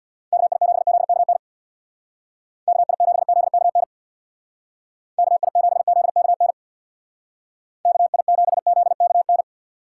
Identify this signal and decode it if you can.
Morse code, heard as a keyed tone, about 59 wpm, 700 Hz. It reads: KI7ZKN KI7ZKN KI7ZKN KI7ZKN
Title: Speed Racing - USA Call Signs 50wpm